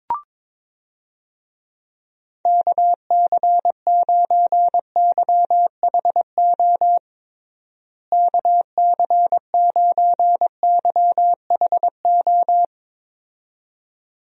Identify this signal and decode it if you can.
Morse code, heard as a keyed tone, about 22 wpm, 700 Hz. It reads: KC9YHO KC9YHO
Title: USA Call Signs - 6 Characters - Repeated 2x - 22wpm